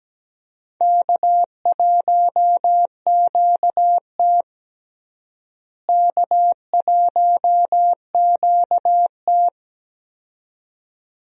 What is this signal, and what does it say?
Morse code, heard as a keyed tone, about 17 wpm, 700 Hz. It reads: K1QT K1QT
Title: USA Call Signs - 4 Characters - Repeated 2x - 17wpm